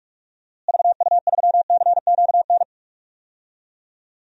Morse code, heard as a keyed tone, about 46 wpm, 700 Hz. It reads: VU3CQN
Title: DXCC-100 Call Signs - 6 Characters - Repeated 3x - 45wpm